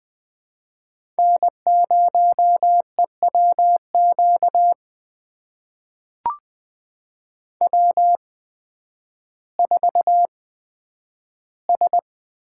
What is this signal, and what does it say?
Morse code, heard as a keyed tone, about 20 wpm, spaced as for 11 wpm, 700 Hz. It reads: N0EWQ W4S
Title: USA Call Signs - 5 Characters - 8x Character Spacing - 20wpm